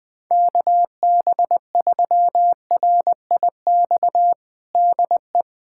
Morse code, heard as a keyed tone, about 20 wpm, 700 Hz. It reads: KB3RIX DE
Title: USA Callsign DE Callsign - 6 Characters - 20wpm